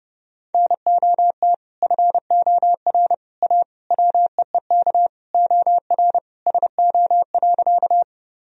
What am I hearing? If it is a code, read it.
Morse code, heard as a keyed tone, about 30 wpm, 700 Hz. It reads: NOT FOR A WEEK OR SO.
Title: Sentences from Top 1000 Words - Review Part 2 - 30wpm